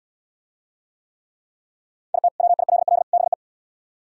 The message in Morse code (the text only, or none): IT WILL BE